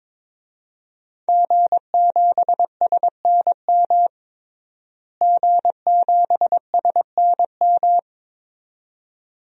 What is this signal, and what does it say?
Morse code, heard as a keyed tone, about 22 wpm, 700 Hz. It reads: G7SNM G7SNM
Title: DXCC-100 Call Signs - 5 Characters - Repeated 2x - 22wpm